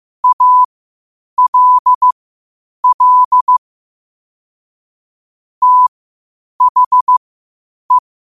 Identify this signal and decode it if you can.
Morse code, heard as a keyed tone, about 15 wpm, spaced as for 8 wpm, 1000 Hz. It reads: ALL THE
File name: qotdfiles / 8wpm